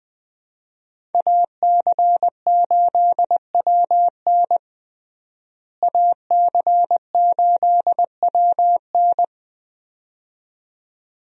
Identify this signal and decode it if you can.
Morse code, heard as a keyed tone, about 20 wpm, 700 Hz. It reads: AC8WN AC8WN
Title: USA Call Signs - 5 Characters - Repeated 2x - 20wpm